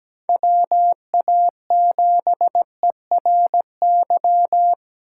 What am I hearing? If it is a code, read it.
Morse code, heard as a keyed tone, about 20 wpm, 700 Hz. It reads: WA7ERY
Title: Speed Racing - USA 2x3 Call Signs 17wpm